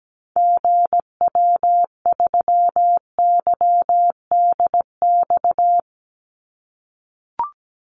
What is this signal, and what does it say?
Morse code, heard as a keyed tone, about 17 wpm, 700 Hz. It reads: GW3YDX
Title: Call Signs - 17wpm